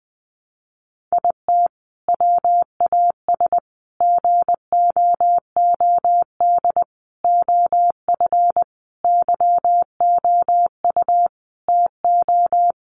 Morse code, heard as a keyed tone, about 19 wpm, 700 Hz. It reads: IT WAS GOOD OF YOU TO
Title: Sentences from Top 1100 Words 20wpm